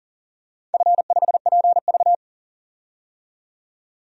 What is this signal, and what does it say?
Morse code, heard as a keyed tone, about 40 wpm, 700 Hz. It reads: F5PV
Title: DXCC-100 Call Signs - 4 Characters - 40wpm